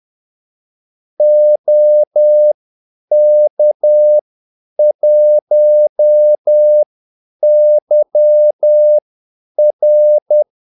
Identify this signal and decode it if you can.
Morse code, heard as a keyed tone, about 10 wpm, spaced as for 8 wpm, 600 Hz. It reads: OK1YR
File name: Callsigns-10-8wpm